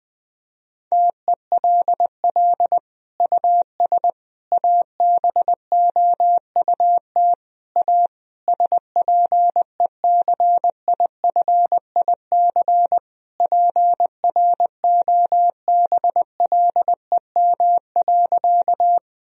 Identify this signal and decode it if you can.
Morse code, heard as a keyed tone, about 21 wpm, 700 Hz. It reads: TELL US ABOUT A SPECIFIC PROBLEM.
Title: Sentences from Top 1000 Words - Review Part 3 - 20wpm